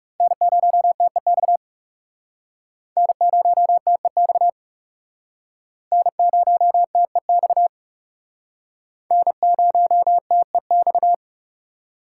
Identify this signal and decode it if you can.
Morse code, heard as a keyed tone, about 37 wpm, 700 Hz. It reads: N0TEX N0TEX N0TEX N0TEX
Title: Speed Racing - USA 1x3 Call Signs 30wpm